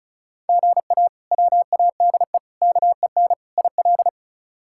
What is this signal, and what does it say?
Morse code, heard as a keyed tone, about 36 wpm, 700 Hz. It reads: GA WADE KEN IL
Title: K1USN SST QSO Practice - 35wpm